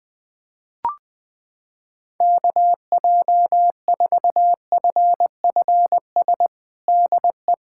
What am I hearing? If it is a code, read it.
Morse code, heard as a keyed tone, about 21 wpm, 700 Hz. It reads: KJ4FFS DE
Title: USA Callsign DE Callsign - 6 Characters - 20wpm